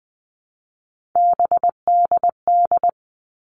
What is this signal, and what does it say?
Morse code, heard as a keyed tone, about 20 wpm, 700 Hz. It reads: BDD
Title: All Cut Numbers - 3-Digit 20wpm